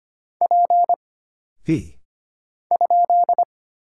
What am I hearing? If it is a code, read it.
Morse code, heard as a keyed tone, about 25 wpm, 700 Hz. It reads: P ?
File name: Single Character (Letter:Number:Punctuation:Prosign) - Mind-Melt - 25wpm